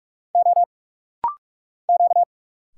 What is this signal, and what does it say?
Morse code, heard as a keyed tone, about 46 wpm, 700 Hz. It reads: O Q
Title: Single Letters - Rapid-Fire 45wpm